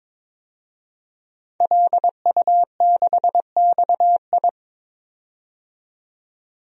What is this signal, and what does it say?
Morse code, heard as a keyed tone, about 22 wpm, 700 Hz. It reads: LU6XI
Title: DXCC-100 Call Signs - 5 Characters - 22wpm